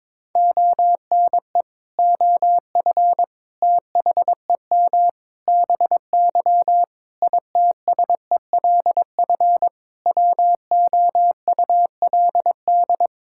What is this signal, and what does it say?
Morse code, heard as a keyed tone, about 22 wpm, 700 Hz. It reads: ONE OF THEM BY ITSELF WOULD
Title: Sentences from Top 1000 Words - Review Part 5 - 22wpm